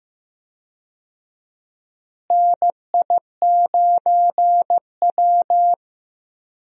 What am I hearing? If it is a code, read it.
Morse code, heard as a keyed tone, about 15 wpm, 700 Hz. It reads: NI9W
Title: USA Call Signs - 4 Characters - Repeated 3x - 15wpm